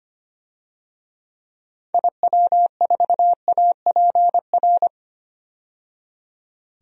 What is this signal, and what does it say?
Morse code, heard as a keyed tone, about 25 wpm, 690 Hz. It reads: IW4APR
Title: DXCC-100 Call Signs - 6 Characters - 25wpm